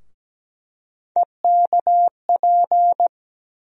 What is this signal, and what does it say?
Morse code, heard as a keyed tone, about 17 wpm, 700 Hz. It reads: EKP
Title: USA Call Sign Postfixes - 3 Letters 17wpm